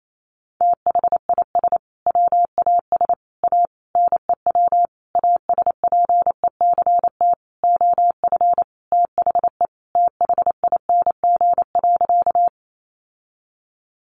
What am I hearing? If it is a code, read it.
Morse code, heard as a keyed tone, about 28 wpm, 700 Hz. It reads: THIS WAS A NEW ASPECT OF THE THING.
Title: Sentences from Top 1300 Words 28wpm